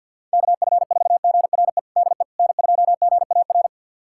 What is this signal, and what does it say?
Morse code, heard as a keyed tone, about 50 wpm, 700 Hz. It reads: KF4GRE DE N2GAR
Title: USA Callsign DE Callsign 50wpm